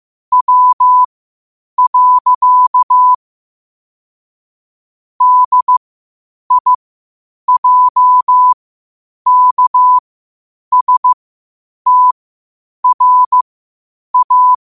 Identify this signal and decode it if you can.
Morse code, heard as a keyed tone, about 15 wpm, spaced as for 8 wpm, 1000 Hz. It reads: W. DIJKSTRA
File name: qotdfiles / 8wpm